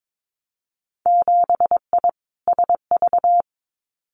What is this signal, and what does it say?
Morse code, heard as a keyed tone, about 22 wpm, 700 Hz. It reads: 7I SV
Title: Winter Field Day Exchanges Encore 22wpm